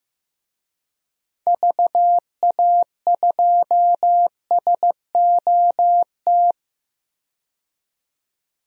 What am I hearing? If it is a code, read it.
Morse code, heard as a keyed tone, about 15 wpm, 700 Hz. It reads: VA2SOT
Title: DXCC-100 Call Signs - 6 Characters - 15wpm